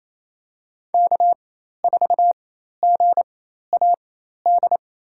Morse code, heard as a keyed tone, about 27 wpm, spaced as for 13 wpm, 700 Hz. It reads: K4GAD
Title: USA Call Signs - 5 Characters - 4x Character Spacing - 28wpm